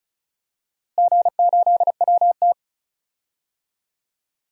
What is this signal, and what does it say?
Morse code, heard as a keyed tone, about 34 wpm, 700 Hz. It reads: G8WT
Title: DXCC-100 Call Signs - 4 Characters - Repeated 3x - 35wpm